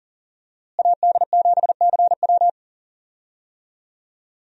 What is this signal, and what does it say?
Morse code, heard as a keyed tone, about 40 wpm, 700 Hz. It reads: AD7CW
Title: USA Call Signs - 5 Characters - Repeated 3x - 40wpm